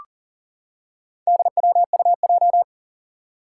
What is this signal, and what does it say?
Morse code, heard as a keyed tone, about 40 wpm, 700 Hz. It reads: DWUJ
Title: Four Challenging Letter-Number Combinations 40wpm